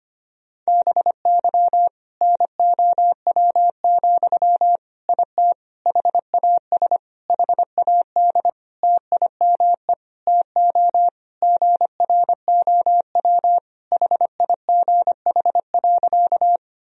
Morse code, heard as a keyed tone, about 25 wpm, 700 Hz. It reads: BY NOW, IT HAS HAD TIME TO GROW HIGH.
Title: Sentences from Top 1000 Words - Review Part 1 - 25wpm